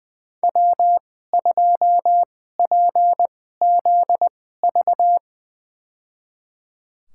Morse code, heard as a keyed tone, about 20 wpm, spaced as for 14 wpm, 700 Hz. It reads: W2PZV
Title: USA Call Signs - 5 Characters - 2x Character Spacing - 20wpm